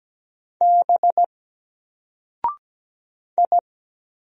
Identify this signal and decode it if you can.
Morse code, heard as a keyed tone, about 17 wpm, 700 Hz. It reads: B I
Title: Single Character (Letter:Number:Punctuation:Prosign) Alt 17wpm